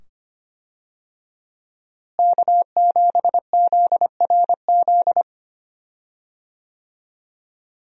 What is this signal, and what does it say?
Morse code, heard as a keyed tone, about 25 wpm, 700 Hz. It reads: K7ZRZ
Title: USA Call Signs - 5 Characters - Repeated 2x - 25wpm